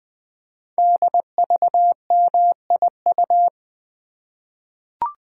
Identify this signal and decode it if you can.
Morse code, heard as a keyed tone, about 20 wpm, 700 Hz. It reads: DVMIU
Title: Code Groups - 5 Letters - 20wpm